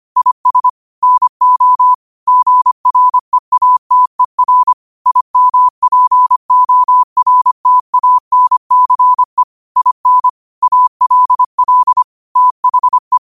Morse code, heard as a keyed tone, about 25 wpm, 1000 Hz. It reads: IS NO GREATER IMPORTANCE IN ALL THE
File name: qotdfiles / 25wpm